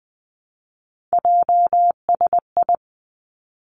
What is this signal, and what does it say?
Morse code, heard as a keyed tone, about 20 wpm, 700 Hz. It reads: JSI
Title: Three Challenging Letter-Number Combinations 20wpm